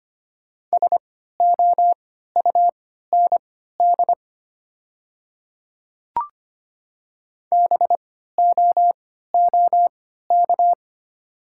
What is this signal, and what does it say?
Morse code, heard as a keyed tone, about 25 wpm, spaced as for 14 wpm, 700 Hz. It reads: SOUND BOOK
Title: 3x Character Spacing - Top 500 Words - 25wpm